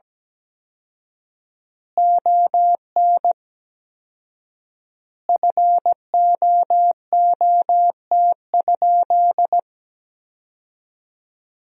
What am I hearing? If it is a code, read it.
Morse code, heard as a keyed tone, about 17 wpm, 700 Hz. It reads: ON FOOT?
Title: 4x Word-Spacing - Sentences from Top 500 Words - 17wpm